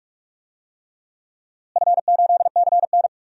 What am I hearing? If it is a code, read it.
Morse code, heard as a keyed tone, about 46 wpm, 700 Hz. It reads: F8CN
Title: DXCC-100 Call Signs - 4 Characters - Repeated 3x - 45wpm